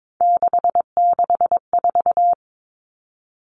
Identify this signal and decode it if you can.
Morse code, heard as a keyed tone, about 22 wpm, 700 Hz. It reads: 664
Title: Common Cut Numbers - 3-Digit 22wpm